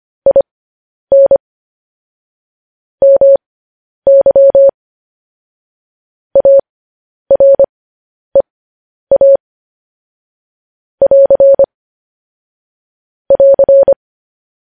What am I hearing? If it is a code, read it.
Morse code, heard as a keyed tone, about 25 wpm, spaced as for 10 wpm, 550 Hz. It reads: IN MY AREA <AR> <AR>
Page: news-headlines-in-morse-code-at-10-wpm